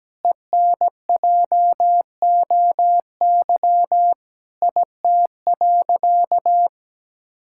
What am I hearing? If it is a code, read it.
Morse code, heard as a keyed tone, about 17 wpm, 700 Hz. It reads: ENJOY IT.
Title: Sentences from Top 1000 Words - Review Part 5 - 17wpm